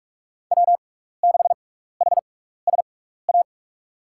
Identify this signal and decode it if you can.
Morse code, heard as a keyed tone, about 46 wpm, 700 Hz. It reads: W 6 H S A
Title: USA Call Signs - 5 Characters - 6x Character Spacing - 45wpm